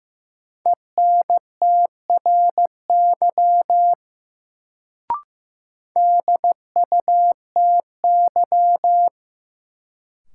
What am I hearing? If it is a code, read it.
Morse code, heard as a keyed tone, about 15 wpm, 700 Hz. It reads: ENTRY DUTY
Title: Sentences from Top 1500 Words - New Word Drill - 15wpm